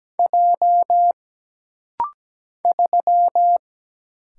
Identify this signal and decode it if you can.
Morse code, heard as a keyed tone, about 17 wpm, 700 Hz. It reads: J 3
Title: Single Letter-Number Alt - Rapid-Fire 17wpm